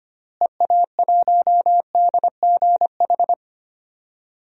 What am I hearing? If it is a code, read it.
Morse code, heard as a keyed tone, about 25 wpm, 700 Hz. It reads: EA1DGH
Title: DXCC-100 Call Signs - 6 Characters - Repeated 3x - 25wpm